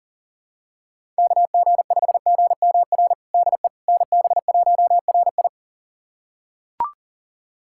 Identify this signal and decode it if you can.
Morse code, heard as a keyed tone, about 40 wpm, 700 Hz. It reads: KG5GMR DE NB1RI
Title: USA Callsign DE Callsign 40wpm